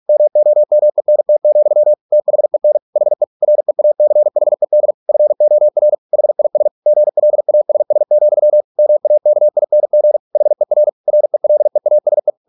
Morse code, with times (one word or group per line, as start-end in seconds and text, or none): 0.1-1.9 MOMENT,
2.1-2.8 THEN
3.0-3.2 HE
3.4-4.9 REACHED
5.1-5.9 FOR
6.1-6.7 HIS
6.9-8.6 GLASS,
8.8-10.2 MAKING
10.4-10.9 HER
11.1-12.3 RELEASE